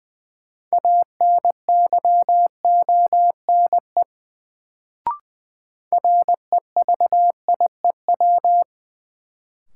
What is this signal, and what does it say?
Morse code, heard as a keyed tone, about 20 wpm, 700 Hz. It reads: ANYONE REVIEW
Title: Sentences from Top 900 Words - New Word Drill - 20wpm